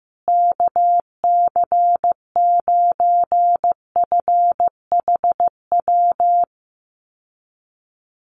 Morse code, heard as a keyed tone, about 15 wpm, 700 Hz. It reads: KC9FHW
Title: USA Call Signs Repeated 3x 15wpm